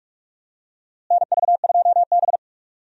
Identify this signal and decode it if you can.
Morse code, heard as a keyed tone, about 46 wpm, 700 Hz. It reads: NV2B
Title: USA Call Signs - 4 Characters - Repeated 2x - 45wpm